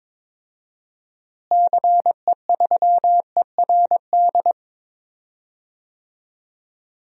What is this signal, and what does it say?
Morse code, heard as a keyed tone, about 22 wpm, 700 Hz. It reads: CE3ERD
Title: DXCC-100 Call Signs - 6 Characters - Repeated 2x - 22wpm